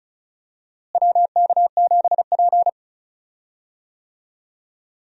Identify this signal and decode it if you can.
Morse code, heard as a keyed tone, about 36 wpm, 700 Hz. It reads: WK7P
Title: USA Call Signs - 4 Characters - Repeated 2x - 35wpm